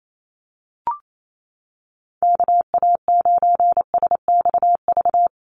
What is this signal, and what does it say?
Morse code, heard as a keyed tone, about 28 wpm, 700 Hz. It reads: KA9SXV
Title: Call Signs Repeated 3x - 28wpm